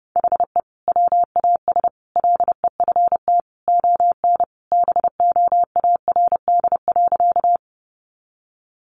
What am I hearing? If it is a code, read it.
Morse code, heard as a keyed tone, about 30 wpm, 700 Hz. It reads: HE WAS LEFT ON BOARD.
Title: Sentences from Top 700 Words 30wpm